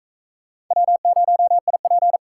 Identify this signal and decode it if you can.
Morse code, heard as a keyed tone, about 42 wpm, 700 Hz. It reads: W0IP